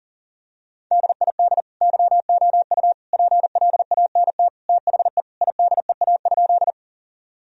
Morse code, heard as a keyed tone, about 40 wpm, 700 Hz. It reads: DID YOU PLANT THE IDEA?
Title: Sentences from Top 1000 Words - Review Part 1 - 40wpm